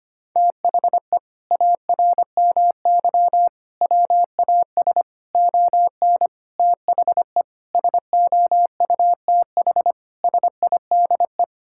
Morse code, heard as a keyed tone, about 25 wpm, 700 Hz. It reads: THE ARMY WAS ON THE SOUTH SIDE
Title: Sentences from Top 1000 Words - Review Part 5 - 25wpm